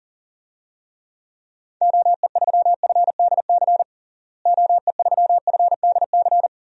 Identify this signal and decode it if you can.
Morse code, heard as a keyed tone, about 40 wpm, 700 Hz. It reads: OE3FDC OE3FDC
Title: DXCC-100 Call Signs - 6 Characters - Repeated 2x - 40wpm